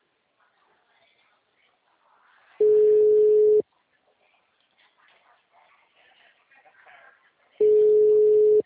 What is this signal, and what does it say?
Ringback tone (Europe, 425 Hz).